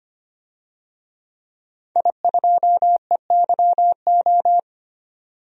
Morse code, heard as a keyed tone, about 25 wpm, 700 Hz. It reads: I2EYO
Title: DXCC-100 Call Signs - 5 Characters - 25wpm